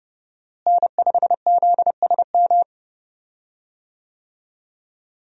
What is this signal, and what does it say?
Morse code, heard as a keyed tone, about 30 wpm, 700 Hz. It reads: N5ZSM
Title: USA Call Signs - 5 Characters - Repeated 3x - 30wpm